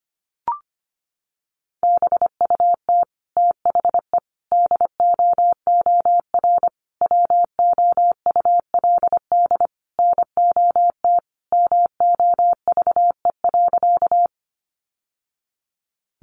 Morse code, heard as a keyed tone, about 25 wpm, 700 Hz. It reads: BUT THE DOOR WOULD NOT MOVE.
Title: Sentences from Top 600 Words 25wpm